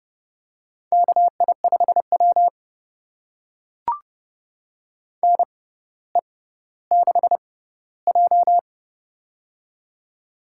Morse code, heard as a keyed tone, about 31 wpm, spaced as for 15 wpm, 700 Hz. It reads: KI5W NE6J
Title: USA Call Signs - 4 Characters - 6x Character Spacing - 30wpm